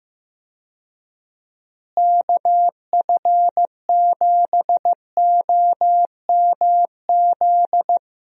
Morse code, heard as a keyed tone, about 16 wpm, 700 Hz. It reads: KF7OMZ
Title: USA Call Signs - 6 Characters - Repeated 2x - 15wpm